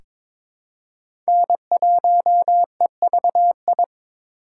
Morse code, heard as a keyed tone, about 22 wpm, 700 Hz. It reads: N1EVI
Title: USA Call Signs - 5 Characters - 4x Character Spacing - 22wpm